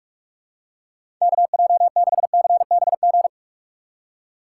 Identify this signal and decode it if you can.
Morse code, heard as a keyed tone, about 47 wpm, 700 Hz. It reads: KJ6CBG
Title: USA Call Signs - 6 Characters - 2x Character Spacing - 45wpm